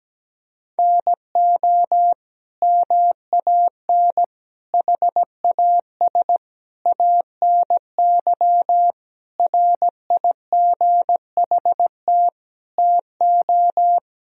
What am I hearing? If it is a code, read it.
Morse code, heard as a keyed tone, about 17 wpm, 700 Hz. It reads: NO MAN HAS ANY RIGHT TO
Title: Sentences from Top 1000 Words - Review Part 2 - 17wpm